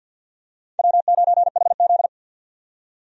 Morse code, heard as a keyed tone, about 52 wpm, 700 Hz. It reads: W9HZ
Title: USA Call Signs - 4 Characters - 50wpm